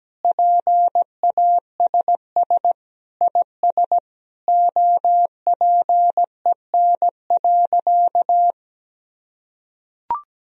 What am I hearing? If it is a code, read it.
Morse code, heard as a keyed tone, about 17 wpm, 700 Hz. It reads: PASS IS OPEN.
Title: Sentences from Top 1000 Words - Review Part 1 - 17wpm